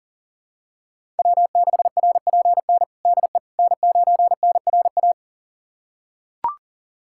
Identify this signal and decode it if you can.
Morse code, heard as a keyed tone, about 37 wpm, 700 Hz. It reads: W6RPN DE N9NRA
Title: USA Callsign DE Callsign - 5 Characters - 40wpm